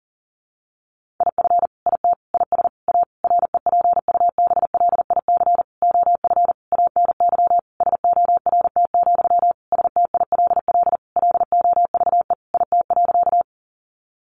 Morse code, heard as a keyed tone, about 40 wpm, 700 Hz. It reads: IF IT IS A REPUBLIC OF ANY SORT, STILL LOVE IT.
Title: Sentences from Top 1500 Words 40wpm